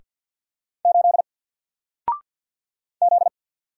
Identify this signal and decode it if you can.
Morse code, heard as a keyed tone, about 50 wpm, 700 Hz. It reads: 8 Z
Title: Single Letter-Number 50wpm